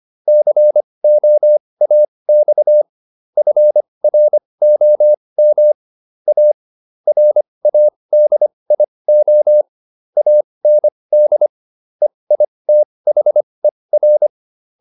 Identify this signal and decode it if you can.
Morse code, heard as a keyed tone, about 25 wpm, spaced as for 20 wpm, 600 Hz. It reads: COAX FROM A RADIO AND EITHER